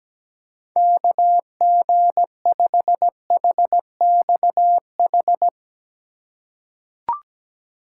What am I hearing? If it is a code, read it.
Morse code, heard as a keyed tone, about 17 wpm, 700 Hz. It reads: KG5HXH
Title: USA Call Signs - 8x Character Spacing - 17wpm